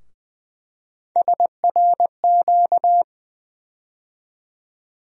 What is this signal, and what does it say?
Morse code, heard as a keyed tone, about 20 wpm, 700 Hz. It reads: SRQ
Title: USA Call Sign Postfixes - 3 Letters 20wpm